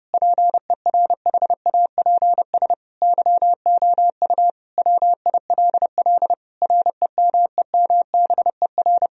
Morse code, heard as a keyed tone, about 30 wpm, 700 Hz. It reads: PERHAPS YOU WILL REMEMBER
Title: Sentences from Top 1000 Words - Review Part 1 - 30wpm